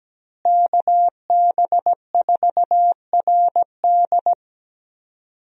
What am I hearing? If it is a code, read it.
Morse code, heard as a keyed tone, about 17 wpm, 700 Hz. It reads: KB4RD
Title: USA Call Signs - 5 Characters - Repeated 2x - 17wpm